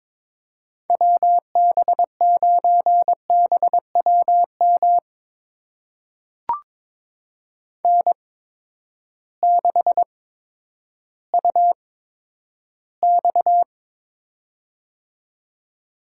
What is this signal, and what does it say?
Morse code, heard as a keyed tone, about 22 wpm, spaced as for 11 wpm, 700 Hz. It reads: WB9BWM N6UX
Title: USA Call Signs - 8x Character Spacing - 22wpm